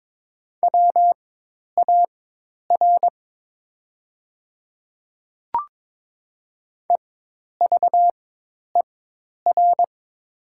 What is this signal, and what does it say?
Morse code, heard as a keyed tone, about 22 wpm, spaced as for 10 wpm, 700 Hz. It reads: WAR EVER
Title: 4x Character Spacing - Top 500 Words - 22wpm